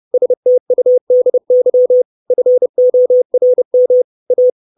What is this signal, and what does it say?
Morse code, heard as a keyed tone, about 30 wpm, 500 Hz. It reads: STUDY FORM A
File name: 632 Words 30WPM